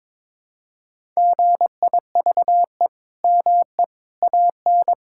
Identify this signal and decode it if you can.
Morse code, heard as a keyed tone, about 22 wpm, 700 Hz. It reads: GIVE ME AN
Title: Sentences from Top 1000 Words - Review Part 5 - 22wpm